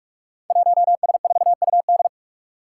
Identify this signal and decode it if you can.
Morse code, heard as a keyed tone, about 41 wpm, 700 Hz. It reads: 1S4UD
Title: Five Challenging Letter-Number Combinations 45wpm